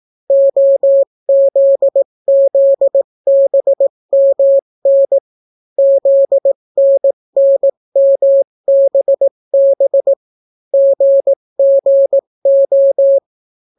Morse code, heard as a keyed tone, about 18 wpm, 550 Hz. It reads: OZZBMN ZNNMBB GGO